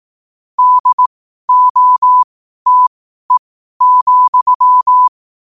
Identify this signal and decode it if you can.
Morse code, heard as a keyed tone, about 18 wpm, spaced as for 12 wpm, 1000 Hz. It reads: DOTE,
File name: qotdfiles / 12wpm